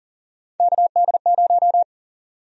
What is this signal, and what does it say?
Morse code, heard as a keyed tone, about 40 wpm, 700 Hz. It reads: KD0
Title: USA Call Sign Prefixes - 40wpm